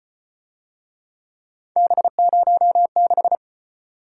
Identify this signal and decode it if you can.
Morse code, heard as a keyed tone, about 33 wpm, 700 Hz. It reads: B06